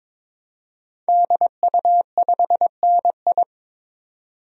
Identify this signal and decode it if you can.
Morse code, heard as a keyed tone, about 22 wpm, 700 Hz. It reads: DU5NI
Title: DXCC-100 Call Signs - 5 Characters - 22wpm